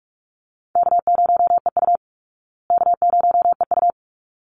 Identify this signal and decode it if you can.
Morse code, heard as a keyed tone, about 44 wpm, 700 Hz. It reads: K0EU K0EU
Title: Call Signs Repeated 3x - 45wpm